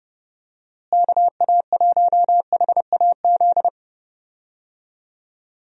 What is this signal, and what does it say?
Morse code, heard as a keyed tone, about 30 wpm, 700 Hz. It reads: KA1HAZ
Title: USA Call Signs - 6 Characters - Repeated 2x - 30wpm